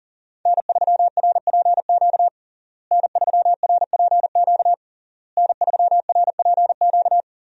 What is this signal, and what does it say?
Morse code, heard as a keyed tone, about 37 wpm, 700 Hz. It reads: N3RPQ N3RPQ N3RPQ
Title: USA Call Signs - 5 Characters - Repeated 3x - 40wpm